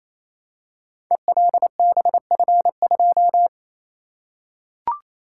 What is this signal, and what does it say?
Morse code, heard as a keyed tone, about 27 wpm, 700 Hz. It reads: ELBF2
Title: Code Groups - 5 Letter:Numbers - 28wpm